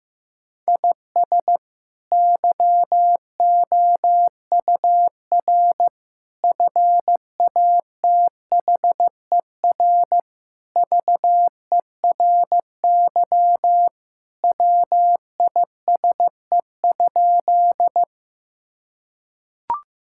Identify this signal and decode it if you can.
Morse code, heard as a keyed tone, about 15 wpm, 700 Hz. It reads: IS YOUR FATHER VERY WISE?
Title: Sentences from Top 1600 Words 15wpm